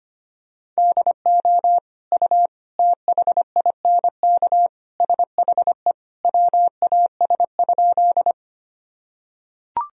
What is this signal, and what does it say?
Morse code, heard as a keyed tone, about 25 wpm, 700 Hz. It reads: DO U THINK SHE WAS?
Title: Sentences from Top 200 Words and Abbreviations 25wpm